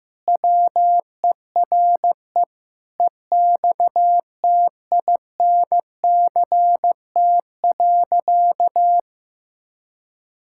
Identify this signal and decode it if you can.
Morse code, heard as a keyed tone, about 15 wpm, 700 Hz. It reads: WERE EXTINCT.
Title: Pangram Challenge v2 - 15wpm